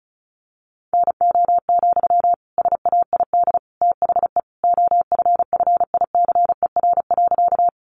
Morse code, heard as a keyed tone, about 34 wpm, 700 Hz. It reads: NO, SAID THE OFFICER.
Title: Sentences from Top 1200 Words 35wpm